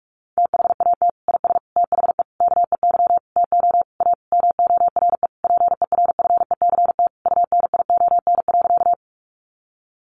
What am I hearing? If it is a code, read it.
Morse code, heard as a keyed tone, about 44 wpm, 700 Hz. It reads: THAT IS THE KEY TO A MORE PERFECT UNION.
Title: Sentences from Top 1400 Words 45wpm